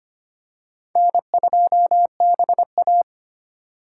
Morse code, heard as a keyed tone, about 25 wpm, 700 Hz. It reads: N2BA
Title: USA Call Signs - 4 Characters - 3x Character Spacing - 25wpm